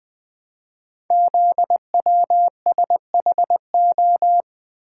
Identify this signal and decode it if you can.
Morse code, heard as a keyed tone, about 20 wpm, 700 Hz. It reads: ZWSHO
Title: Code Groups - 5 Letters - 20wpm